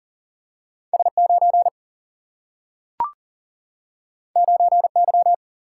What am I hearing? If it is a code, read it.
Morse code, heard as a keyed tone, about 40 wpm, 700 Hz. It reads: S9 9Y
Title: Single-Letter Single-Number 40wpm